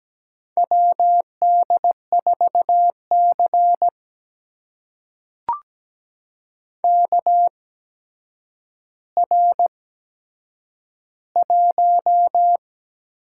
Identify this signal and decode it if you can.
Morse code, heard as a keyed tone, about 17 wpm, spaced as for 8 wpm, 700 Hz. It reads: WD4C KR1
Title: USA Call Signs - 4 Characters - 8x Character Spacing - 17wpm